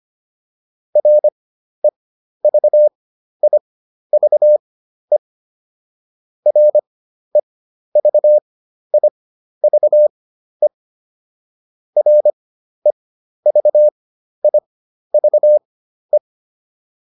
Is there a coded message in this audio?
Yes: Morse code from 1.0 to 16.2 s, heard as a keyed tone, about 25 wpm, spaced as for 12 wpm, 600 Hz. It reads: REVIVE REVIVE REVIVE